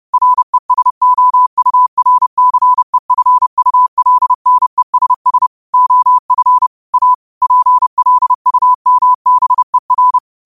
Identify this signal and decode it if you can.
Morse code, heard as a keyed tone, about 30 wpm, 1000 Hz. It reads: RESOURCEFULNESS OF A PLUMBER